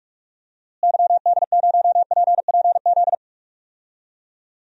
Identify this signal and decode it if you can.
Morse code, heard as a keyed tone, about 41 wpm, 700 Hz. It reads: YD0PPZ
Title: DXCC-100 Call Signs - 6 Characters - Repeated 3x - 45wpm